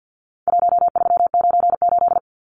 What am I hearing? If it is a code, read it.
Morse code, heard as a keyed tone, about 50 wpm, 700 Hz. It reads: 1398
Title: Numbers - Contest Style - 4-Digit 50wpm